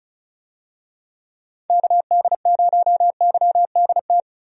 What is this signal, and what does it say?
Morse code, heard as a keyed tone, about 35 wpm, 700 Hz. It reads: KD0YDT
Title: USA Call Signs - 6 Characters - 35wpm